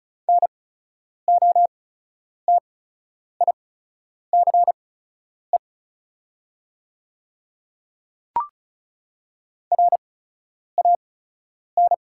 Morse code, heard as a keyed tone, about 36 wpm, spaced as for 10 wpm, 690 Hz. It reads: NOTICE RAN
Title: 8x Character Spacing - Top 500 Words - 35wpm